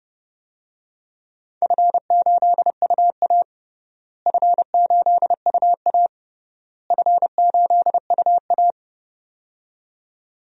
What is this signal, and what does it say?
Morse code, heard as a keyed tone, about 32 wpm, 700 Hz. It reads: F8UA F8UA F8UA
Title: DXCC-100 Call Signs - 4 Characters - Repeated 3x - 30wpm